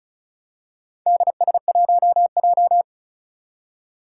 Morse code, heard as a keyed tone, about 36 wpm, 690 Hz. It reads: DS1J
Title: Four Challenging Letter-Number Combinations 35wpm